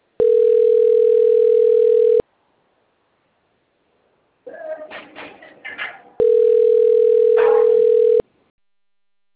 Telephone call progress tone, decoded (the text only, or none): ringback tone